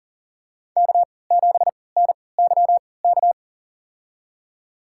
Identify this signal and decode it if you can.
Morse code, heard as a keyed tone, about 40 wpm, spaced as for 23 wpm, 700 Hz. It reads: K7NYK
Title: USA Call Signs - 3x Character Spacing - 40wpm